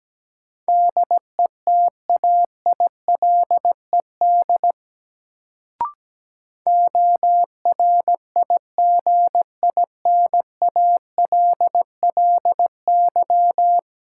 Morse code, heard as a keyed tone, about 17 wpm, 700 Hz. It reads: DETAILED ORIGINALLY
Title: Sentences from Top 1500 Words - New Word Drill Alt - 17wpm